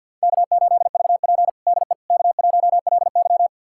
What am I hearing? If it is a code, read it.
Morse code, heard as a keyed tone, about 52 wpm, 700 Hz. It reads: K8VP DE K1LY
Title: USA Callsign DE Callsign - 4 Characters - 50wpm